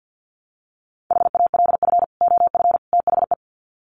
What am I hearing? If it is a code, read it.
Morse code, heard as a keyed tone, about 48 wpm, 700 Hz. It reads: HALF OF THE
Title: Sets of 3 Words 50wpm